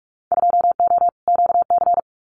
Morse code, heard as a keyed tone, about 46 wpm, 700 Hz. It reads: 2O QC
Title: Winter Field Day Exchanges 45wpm